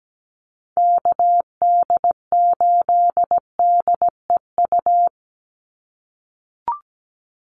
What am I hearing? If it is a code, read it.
Morse code, heard as a keyed tone, about 17 wpm, 700 Hz. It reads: KD8DEU
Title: NA SOTA Chaser Callsigns 17wpm